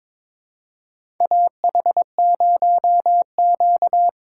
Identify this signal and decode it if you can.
Morse code, heard as a keyed tone, about 22 wpm, 700 Hz. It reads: AH0Q
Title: USA Call Signs - 4x Character Spacing - 22wpm